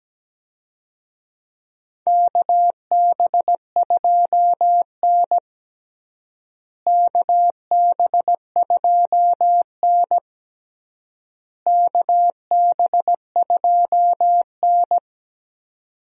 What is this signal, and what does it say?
Morse code, heard as a keyed tone, about 17 wpm, 700 Hz. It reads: KB2N KB2N KB2N
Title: USA Call Signs - 4 Characters - Repeated 3x - 17wpm